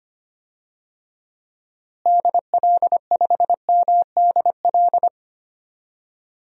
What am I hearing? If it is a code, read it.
Morse code, heard as a keyed tone, about 25 wpm, 700 Hz. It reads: DL5MDL
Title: DXCC-100 Call Signs - 6 Characters - Repeated 2x - 25wpm